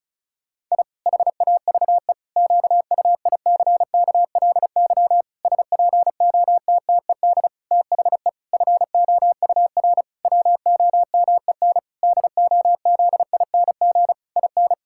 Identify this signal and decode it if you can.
Morse code, heard as a keyed tone, about 36 wpm, 700 Hz. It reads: I HAVE QUICKLY SPOTTED THE FOUR WOMEN DOZING IN
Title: Pangram Challenge v2 - 35wpm